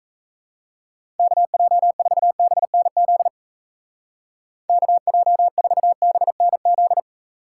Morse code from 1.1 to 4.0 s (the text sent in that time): KJ4BNZ